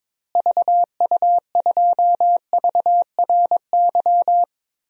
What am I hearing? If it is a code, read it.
Morse code, heard as a keyed tone, about 22 wpm, 700 Hz. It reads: VU2VRY
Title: DXCC-100 Call Signs - 6 Characters - Repeated 2x - 22wpm